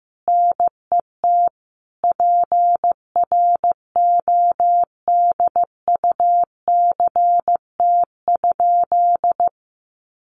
Morse code, heard as a keyed tone, about 15 wpm, 700 Hz. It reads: NET PRODUCT?
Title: Sentences from Top 1400 Words 15wpm